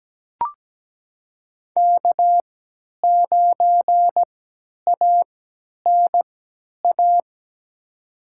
Morse code, heard as a keyed tone, about 17 wpm, spaced as for 10 wpm, 700 Hz. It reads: K9ANA
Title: USA Call Signs - 5 Characters - 3x Character Spacing - 17wpm